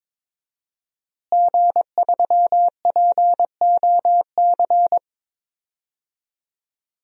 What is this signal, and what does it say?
Morse code, heard as a keyed tone, about 22 wpm, 700 Hz. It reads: G3POC
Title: DXCC-100 Call Signs - 5 Characters - Repeated 3x - 22wpm